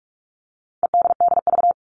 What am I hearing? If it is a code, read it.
Morse code, heard as a keyed tone, about 44 wpm, 700 Hz. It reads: EDDV
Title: All Cut Numbers - 4-Digit 45wpm